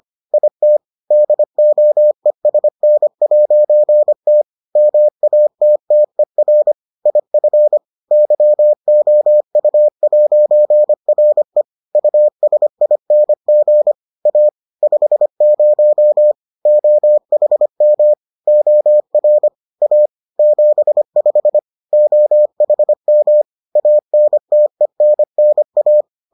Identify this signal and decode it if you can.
Morse code, heard as a keyed tone, about 25 wpm, 600 Hz. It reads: IT DOESN'T MATTER IF YOU'RE USING A 50 OHM OR A 75 OHM ANTENNA